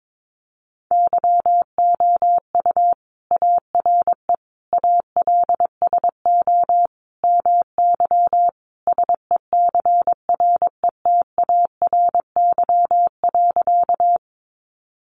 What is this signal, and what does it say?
Morse code, heard as a keyed tone, about 22 wpm, 700 Hz. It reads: YOU ARE ALSO MY SECRETARY.
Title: Sentences from Top 1300 Words 22wpm